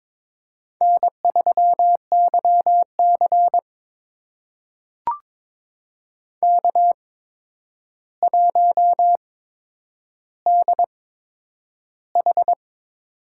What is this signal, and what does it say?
Morse code, heard as a keyed tone, about 22 wpm, spaced as for 9 wpm, 700 Hz. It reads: N3YC K1DH
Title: USA Call Signs - 4 Characters - 8x Character Spacing - 22wpm